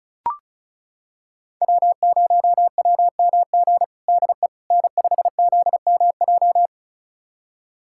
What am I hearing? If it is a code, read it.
Morse code, heard as a keyed tone, about 36 wpm, 700 Hz. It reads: W0WMG DE N5ZMJ
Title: USA Callsign DE Callsign - 5 Characters - 35wpm